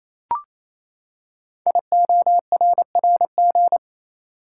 Morse code, heard as a keyed tone, about 28 wpm, 700 Hz. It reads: IORRG
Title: Code Groups - 5 Letters - 28wpm